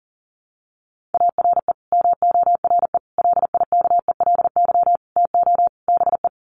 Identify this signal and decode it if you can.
Morse code, heard as a keyed tone, about 40 wpm, 700 Hz. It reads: ARE MORE LIKELY TO BE
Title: Sets of 5 Words 40wpm